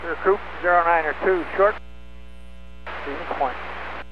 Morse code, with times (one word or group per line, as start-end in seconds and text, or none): none